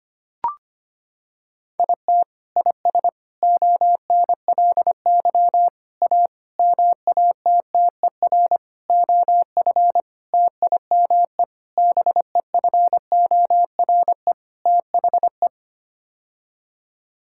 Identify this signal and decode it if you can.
Morse code, heard as a keyed tone, about 25 wpm, 700 Hz. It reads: IT IS ONLY A MATTER OF TIME BEFORE THE
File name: Sets of 9 Words - 25wpm